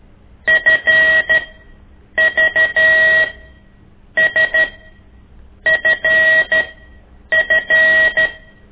Morse code, heard as a keyed tone, about 13 wpm, spaced as for 7 wpm, 1900 Hz. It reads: FVSFF
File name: ik1weg2 / CORSI CW